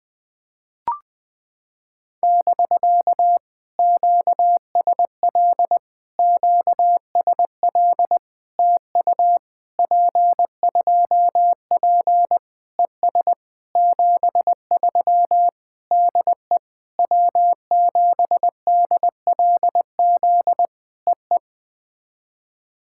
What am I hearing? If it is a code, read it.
Morse code, heard as a keyed tone, about 19 wpm, 700 Hz. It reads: <BK> QSL QSL TU P2P ES 73 DE W7DLZ EE
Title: POTA QSO Practice - 20wpm